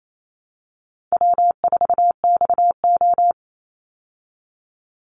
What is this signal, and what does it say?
Morse code, heard as a keyed tone, about 28 wpm, 700 Hz. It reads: W4XO
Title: Easy Call Signs Repeated 2x - 28wpm